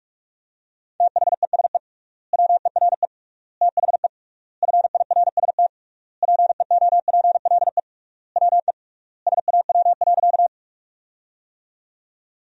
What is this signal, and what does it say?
Morse code, heard as a keyed tone, about 47 wpm, 700 Hz. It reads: THESE WERE THE FIRST PEOPLE WE SAW.
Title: 3x Word-Spacing - Sentences from Top 300 Words - 45wpm